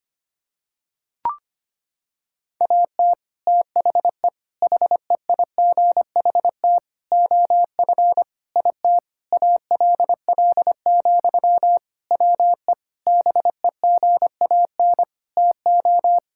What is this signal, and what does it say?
Morse code, heard as a keyed tone, about 25 wpm, 700 Hz. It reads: AT THE HEIGHT OF IT ALL, WE BEGAN TO
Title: Sentences from Top 1600 Words 25wpm